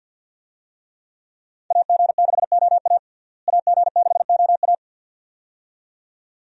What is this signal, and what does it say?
Morse code, heard as a keyed tone, about 47 wpm, 700 Hz. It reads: AG6OA AG6OA
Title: USA Call Signs - 5 Characters - Repeated 2x - 50wpm